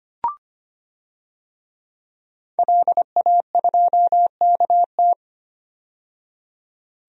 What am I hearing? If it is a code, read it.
Morse code, heard as a keyed tone, about 25 wpm, 700 Hz. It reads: LA2KT
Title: DXCC-100 Call Signs - 5 Characters - 25wpm